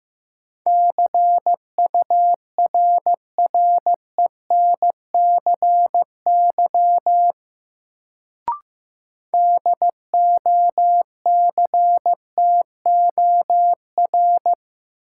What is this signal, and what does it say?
Morse code, heard as a keyed tone, about 15 wpm, 700 Hz. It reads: CURRENCY DOCTOR
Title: Sentences from Top 1500 Words - New Word Drill - 15wpm